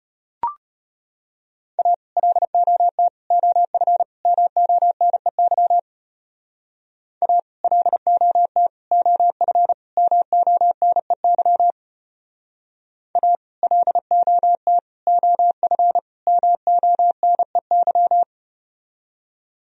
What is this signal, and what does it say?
Morse code, heard as a keyed tone, about 33 wpm, 700 Hz. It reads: A LOT OF MONEY A LOT OF MONEY A LOT OF MONEY